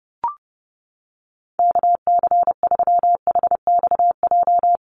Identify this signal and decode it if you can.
Morse code, heard as a keyed tone, about 30 wpm, 700 Hz. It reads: KC3HXJ
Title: USA Call Signs Repeated 3x 30wpm